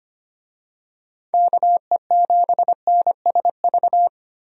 Morse code, heard as a keyed tone, about 25 wpm, 700 Hz. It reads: KE7NSV
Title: USA Call Signs - 6 Characters - Repeated 2x - 25wpm